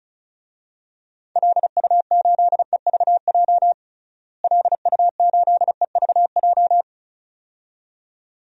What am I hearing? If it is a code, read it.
Morse code, heard as a keyed tone, about 34 wpm, 700 Hz. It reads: LU8EVJ LU8EVJ
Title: DXCC-100 Call Signs - 6 Characters - Repeated 2x - 35wpm